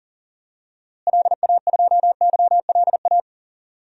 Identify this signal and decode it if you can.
Morse code, heard as a keyed tone, about 40 wpm, 700 Hz. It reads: LA2YLA
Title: DXCC-100 Call Signs - 6 Characters - Repeated 3x - 40wpm